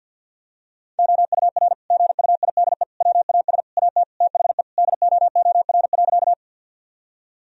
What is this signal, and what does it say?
Morse code, heard as a keyed tone, about 50 wpm, 700 Hz. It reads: OUR GUIDE WAS AT THE DOOR.
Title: Sentences from Top 1000 Words - Review Part 3 - 50wpm